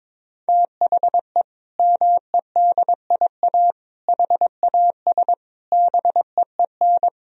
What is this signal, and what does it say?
Morse code, heard as a keyed tone, about 21 wpm, 700 Hz. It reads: THE MEDIA HAS BEEN
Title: Sentences from Top 1000 Words - Review Part 4 - 22wpm